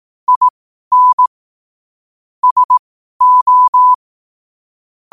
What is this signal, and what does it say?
Morse code, heard as a keyed tone, about 18 wpm, spaced as for 12 wpm, 1000 Hz. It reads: IN SO
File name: qotdfiles / 12wpm